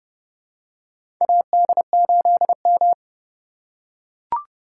Morse code, heard as a keyed tone, about 32 wpm, 700 Hz. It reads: AD8M
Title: USA Call Signs - 4 Characters - 4x Character Spacing - 30wpm